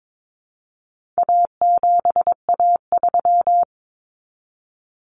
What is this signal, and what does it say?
Morse code, heard as a keyed tone, about 22 wpm, 700 Hz. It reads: A7A3
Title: Common Cut Numbers - 4-Digit 22wpm